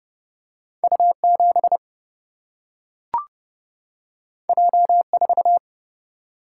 Morse code, heard as a keyed tone, about 30 wpm, 700 Hz. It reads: U 7 J 4
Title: Single-Letter Single-Number 30wpm